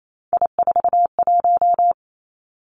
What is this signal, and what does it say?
Morse code, heard as a keyed tone, about 28 wpm, 700 Hz. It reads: I41
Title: Three Challenging Letter-Number Combinations 28wpm